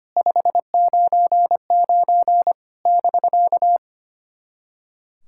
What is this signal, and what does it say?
Morse code, heard as a keyed tone, about 25 wpm, 690 Hz. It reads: 599 <BK>
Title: SOTA QSO Practice 25wpm